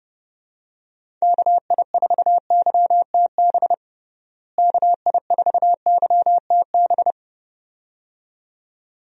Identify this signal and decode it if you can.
Morse code, heard as a keyed tone, about 30 wpm, 700 Hz. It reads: KI4YTB KI4YTB
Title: USA Call Signs - 6 Characters - Repeated 2x - 30wpm